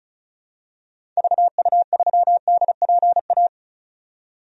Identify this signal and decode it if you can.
Morse code, heard as a keyed tone, about 34 wpm, 700 Hz. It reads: VU3DPA
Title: DXCC-100 Call Signs - 6 Characters - 35wpm